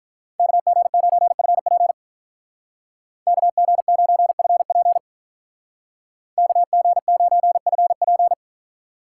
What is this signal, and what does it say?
Morse code, heard as a keyed tone, about 46 wpm, 700 Hz. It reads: KG9FP KG9FP KG9FP